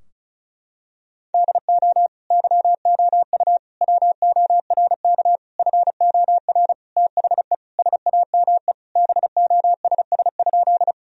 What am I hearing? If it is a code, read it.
Morse code, heard as a keyed tone, about 36 wpm, 700 Hz. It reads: DO YOU WORK FOR THE SAME BOSS?
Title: Sentences from Top 1000 Words - Review Part 4 - 35wpm